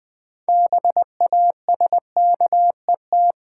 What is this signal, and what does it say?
Morse code, heard as a keyed tone, about 20 wpm, 700 Hz. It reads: BASKET
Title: Six Letter Words - 20wpm